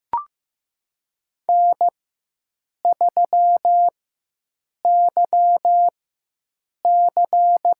Morse code, heard as a keyed tone, about 15 wpm, spaced as for 7 wpm, 700 Hz. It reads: N3YC
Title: USA Call Signs - 4 Characters - 4x Character Spacing - 15wpm